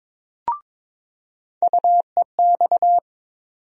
Morse code, heard as a keyed tone, about 22 wpm, 700 Hz. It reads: UEX
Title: USA Call Sign Postfixes - 3 Letters 22wpm